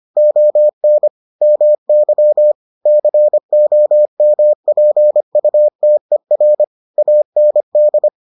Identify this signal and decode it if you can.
Morse code, heard as a keyed tone, about 25 wpm, 600 Hz. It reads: ON MY COMPUTER AND